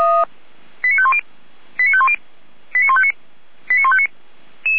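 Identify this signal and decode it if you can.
Five-tone selective call (ZVEI), heard as quick runs of five tones, tones of about 70 ms. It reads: 89510 89510 89159 89159
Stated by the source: ZVEI five-tone selective call received by SDR, from the audio archive of Feuerwehr Orlamünde fire brigade.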